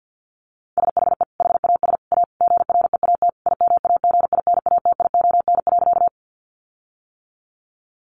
Morse code, heard as a keyed tone, about 50 wpm, 700 Hz. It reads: SHE HAS A GREAT IMAGINATION.
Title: Sentences from Top 1300 Words 50wpm